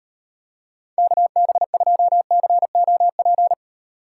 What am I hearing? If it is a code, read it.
Morse code, heard as a keyed tone, about 31 wpm, 700 Hz. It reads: KB2COP
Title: Speed Racing - USA 2x3 Call Signs 25wpm